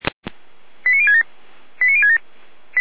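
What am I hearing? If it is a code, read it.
Five-tone selective call (ZVEI), heard as quick runs of five tones, tones of about 70 ms. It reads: 89067 89067
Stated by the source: ZVEI five-tone selective call 89072, received on an SDR.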